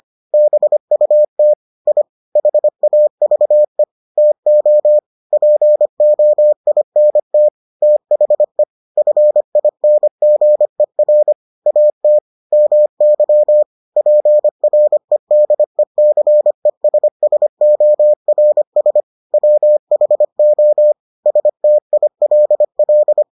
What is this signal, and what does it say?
Morse code, heard as a keyed tone, about 25 wpm, 600 Hz. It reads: BUT I HAVE TO POINT THE FINGER AT MY PREDECESSORS WHO STILL